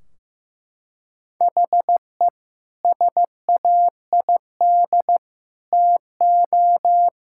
Morse code, heard as a keyed tone, about 15 wpm, 700 Hz. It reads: HE SAID TO
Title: Sentences from Top 1600 Words 15wpm